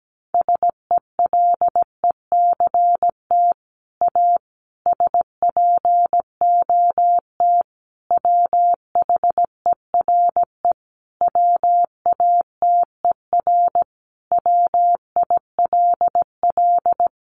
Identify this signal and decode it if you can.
Morse code, heard as a keyed tone, about 17 wpm, 700 Hz. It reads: SELECT A SPOT WHERE WATER WILL
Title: Sentences from Top 1300 Words 17wpm